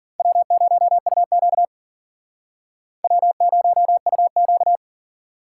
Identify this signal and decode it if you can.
Morse code, heard as a keyed tone, about 51 wpm, 700 Hz. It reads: W0UQ W0UQ